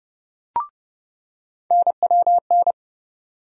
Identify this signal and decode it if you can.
Morse code, heard as a keyed tone, about 30 wpm, 700 Hz. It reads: NWN
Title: USA Call Sign Postfixes - 3 Letters 30wpm